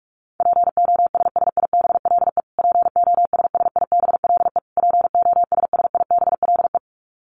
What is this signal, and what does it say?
Morse code, heard as a keyed tone, about 44 wpm, 700 Hz. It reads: POSSIBLE POSSIBLE POSSIBLE
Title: Top 500 Words Repeated 45wpm